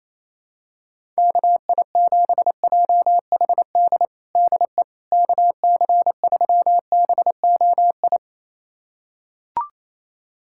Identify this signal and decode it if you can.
Morse code, heard as a keyed tone, about 28 wpm, 700 Hz. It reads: KI7JHD DE KC3BOI
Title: USA Callsign DE Callsign 28wpm